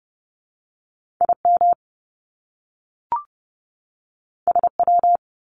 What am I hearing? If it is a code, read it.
Morse code, heard as a keyed tone, about 30 wpm, 700 Hz. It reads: IM SW
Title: Top Two Letter Combinations in Words 30wpm